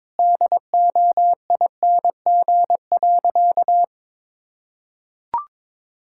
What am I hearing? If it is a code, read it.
Morse code, heard as a keyed tone, about 22 wpm, 700 Hz. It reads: DOING.
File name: practice / 22wpm / Sentences with 4-Word Phrases - 22wpm